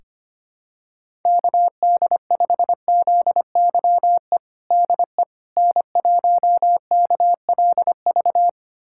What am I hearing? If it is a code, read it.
Morse code, heard as a keyed tone, about 25 wpm, 700 Hz. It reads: KD5ZYE DE N1KLV
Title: USA Callsign DE Callsign 25wpm